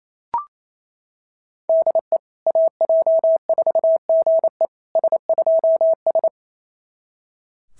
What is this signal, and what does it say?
Morse code, heard as a keyed tone, about 27 wpm, 650 Hz. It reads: DE AJ4GE S2S
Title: SOTA QSO Practice 28wpm